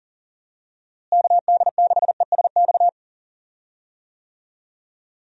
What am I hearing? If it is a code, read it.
Morse code, heard as a keyed tone, about 39 wpm, 700 Hz. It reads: KD6ESX
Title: USA Call Signs - 6 Characters - Repeated 2x - 40wpm